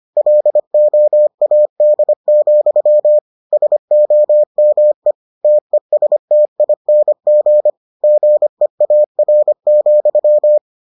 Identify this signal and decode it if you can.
Morse code, heard as a keyed tone, about 25 wpm, 600 Hz. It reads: LOAD, SOME TESTING GEAR,